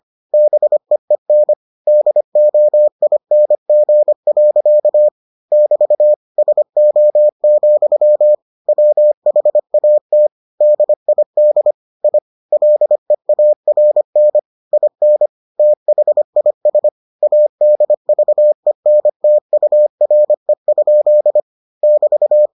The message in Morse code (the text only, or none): BEEN DOING. <BT> SO, WHAT DID I LEARN IN THIS ADVENTURE? <BT>